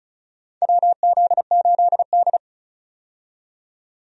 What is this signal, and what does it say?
Morse code, heard as a keyed tone, about 36 wpm, 700 Hz. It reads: WZ8D
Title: USA Call Signs - 4 Characters - 35wpm